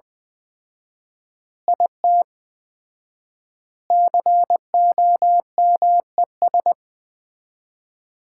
Morse code, heard as a keyed tone, about 20 wpm, 700 Hz. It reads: IT COMES
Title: 4x Word-Spacing - Sets of 5 Words - 20wpm